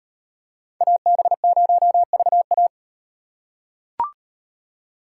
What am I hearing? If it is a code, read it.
Morse code, heard as a keyed tone, about 38 wpm, 700 Hz. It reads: AB0VA